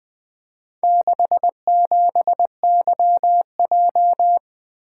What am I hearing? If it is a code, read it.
Morse code, heard as a keyed tone, about 20 wpm, 700 Hz. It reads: 67YJ
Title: Four Letter-Number Combinations - 20wpm